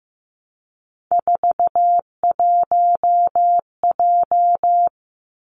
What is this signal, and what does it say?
Morse code, heard as a keyed tone, about 15 wpm, 700 Hz. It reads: 41J
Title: Three Challenging Letter-Number Combinations 15wpm